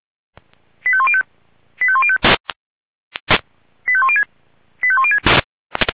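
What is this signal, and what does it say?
Five-tone selective call (ZVEI), heard as quick runs of five tones, tones of about 70 ms. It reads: 85105 85105 85106 85106
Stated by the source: ZVEI five-tone selective call received on an SDR.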